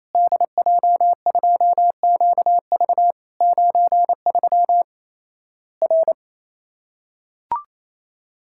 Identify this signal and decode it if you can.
Morse code, heard as a keyed tone, about 28 wpm, 700 Hz. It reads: DJ2QV 93 R
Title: Worked All Europe DX Contest - QTC Exchange - 28wpm